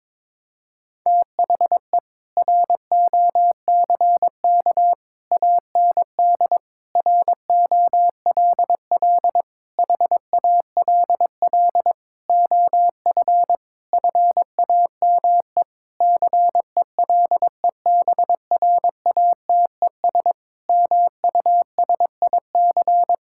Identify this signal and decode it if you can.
Morse code, heard as a keyed tone, about 22 wpm, 700 Hz. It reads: THE ROCK AND ROLL HALL OF FAME CELEBRATES MUSIC
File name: Sentences with 7-Word Phrases - 22wpm